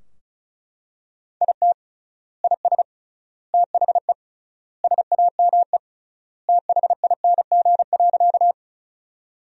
Morse code, heard as a keyed tone, about 36 wpm, 700 Hz. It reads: IT IS THE SAME THING.
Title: 3x Word-Spacing - Sentences from Top 200 Words - 35wpm